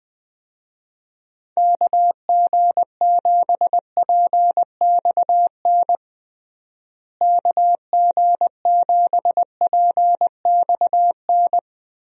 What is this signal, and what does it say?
Morse code, heard as a keyed tone, about 20 wpm, 700 Hz. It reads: KG7PXN KG7PXN
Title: USA Call Signs - 6 Characters - Repeated 2x - 20wpm